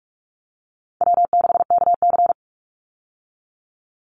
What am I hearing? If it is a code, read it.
Morse code, heard as a keyed tone, about 44 wpm, 700 Hz. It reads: W6KC
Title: Call Signs Repeated 3x - 45wpm